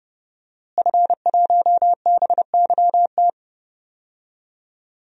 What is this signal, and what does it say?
Morse code, heard as a keyed tone, about 32 wpm, 700 Hz. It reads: F1BYT
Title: DXCC-100 Call Signs - 5 Characters - Repeated 3x - 30wpm